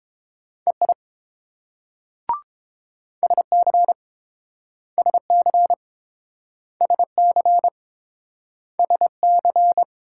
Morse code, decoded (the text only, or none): EI SC SC SC SC